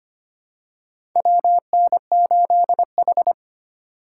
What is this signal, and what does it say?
Morse code, heard as a keyed tone, about 25 wpm, 700 Hz. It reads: WN8H
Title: USA Call Signs - 3x Character Spacing - 25wpm